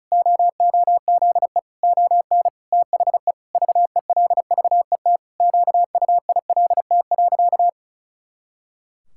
Morse code, heard as a keyed tone, about 34 wpm, 700 Hz. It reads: OOZE ON THE VELVET QUILT.
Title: Pangram Challenge v2 - 35wpm